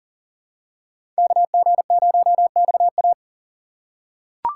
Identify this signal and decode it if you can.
Morse code, heard as a keyed tone, about 40 wpm, 700 Hz. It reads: KG0XA
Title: USA Call Signs - 5 Characters - 6x Character Spacing - 40wpm